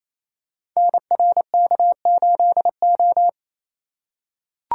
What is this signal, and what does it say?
Morse code, heard as a keyed tone, about 27 wpm, 700 Hz. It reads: NRK8O
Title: Code Groups - 5 Letter:Numbers - 28wpm